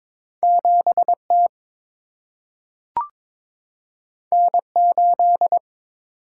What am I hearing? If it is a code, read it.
Morse code, heard as a keyed tone, about 22 wpm, 700 Hz. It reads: 7 T N 8
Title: Single-Letter Single-Number 22wpm